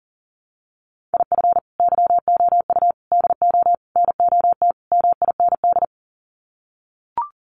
Morse code, heard as a keyed tone, about 37 wpm, 700 Hz. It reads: IF YOU DO NOT MIND
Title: Sets of 5 Words 40wpm